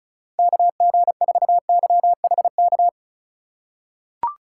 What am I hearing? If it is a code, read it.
Morse code, heard as a keyed tone, about 34 wpm, 700 Hz. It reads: KG4YHK
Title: USA Call Signs - 6 Characters - 3x Character Spacing - 35wpm